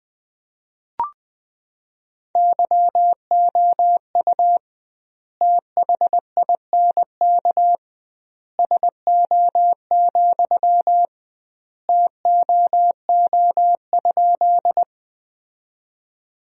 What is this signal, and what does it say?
Morse code, heard as a keyed tone, about 20 wpm, 700 Hz. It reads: YOU THINK SO, TOO?
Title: 2x Word-Spacing - Sentences from Top 200 Words - 20wpm